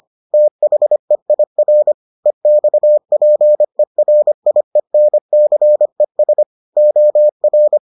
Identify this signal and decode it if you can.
Morse code, heard as a keyed tone, about 25 wpm, 600 Hz. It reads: THEIR EXPERIENCES OR